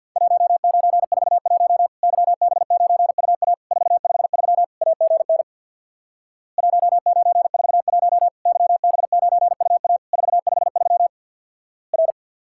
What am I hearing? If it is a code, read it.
Morse code, heard as a keyed tone, about 50 wpm, 700 Hz. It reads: 1941 YB9UA 453 AGN 1941 YB9UA 453 R
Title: Worked All Europe DX Contest - QTC Exchange with Repeats - 50wpm